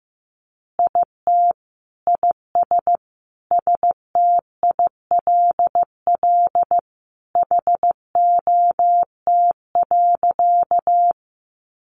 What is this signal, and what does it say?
Morse code, heard as a keyed tone, about 14 wpm, 700 Hz. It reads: IT IS STILL HOT.
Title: Additional Easy Phrases 15wpm